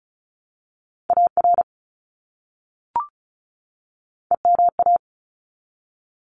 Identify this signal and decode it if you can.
Morse code, heard as a keyed tone, about 34 wpm, 700 Hz. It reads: AR EMA
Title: ARRL Sections 35wpm